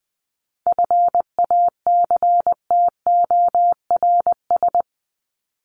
Morse code, heard as a keyed tone, about 20 wpm, 700 Hz. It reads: FACTORS
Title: ES:S Suffix - Top 100 Words with Base Words 20wpm